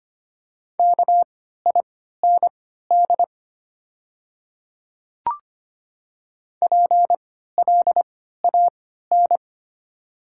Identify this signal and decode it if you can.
Morse code, heard as a keyed tone, about 25 wpm, spaced as for 14 wpm, 690 Hz. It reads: KIND PLAN
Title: 3x Character Spacing - Top 500 Words - 25wpm